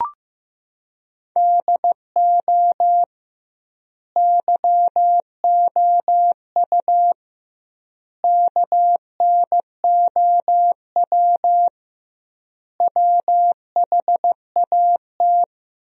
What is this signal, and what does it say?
Morse code, heard as a keyed tone, about 15 wpm, 700 Hz. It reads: DO YOU KNOW WHAT
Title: 2x Word-Spacing - Sentences from Top 100 Words - 15wpm